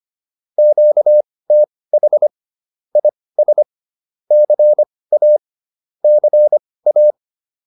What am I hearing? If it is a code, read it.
Morse code, heard as a keyed tone, about 25 wpm, spaced as for 18 wpm, 600 Hz. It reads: QTH IS CA CA
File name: QSO102 18